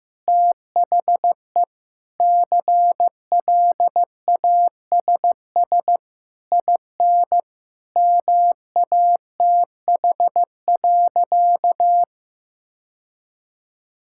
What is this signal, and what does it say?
Morse code, heard as a keyed tone, about 15 wpm, 700 Hz. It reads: THE CLASS IN MATH.
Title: Sentences from Top 1000 Words - Review Part 6 - 15wpm